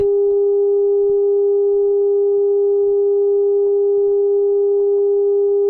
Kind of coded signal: telephone call progress tone